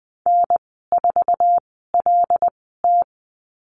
Morse code, heard as a keyed tone, about 20 wpm, spaced as for 14 wpm, 700 Hz. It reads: N4LT
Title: 088 Lesson 15g - Number 5 - Call Signs 20wpm